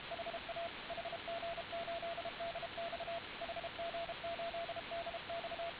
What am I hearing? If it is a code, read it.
Morse code, heard as a keyed tone, about 32 wpm, 680 Hz, with radio noise. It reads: HA HG8DX HG8DX